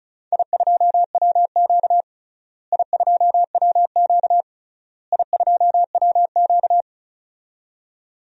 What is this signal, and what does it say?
Morse code, heard as a keyed tone, about 36 wpm, 700 Hz. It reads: I2WQ I2WQ I2WQ
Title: DXCC-100 Call Signs - 4 Characters - Repeated 3x - 35wpm